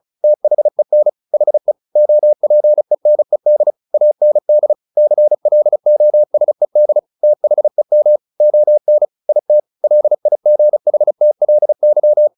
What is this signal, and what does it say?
Morse code, heard as a keyed tone, about 34 wpm, 600 Hz. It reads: THEN HE OPENED AND CLOSED THEM ON IT LIGHTLY
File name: the old man and the sea 2 35wpm